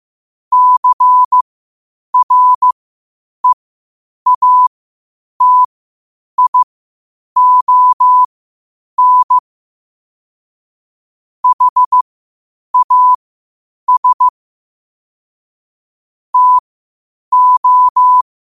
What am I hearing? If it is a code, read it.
Morse code, heard as a keyed tone, about 15 wpm, spaced as for 8 wpm, 1000 Hz. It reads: CREATION HAS TO
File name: qotdfiles / 8wpm